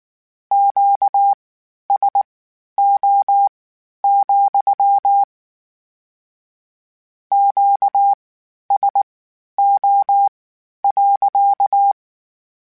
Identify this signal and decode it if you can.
Morse code, heard as a keyed tone, about 19 wpm, spaced as for 11 wpm, 800 Hz. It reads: QSO, QSO.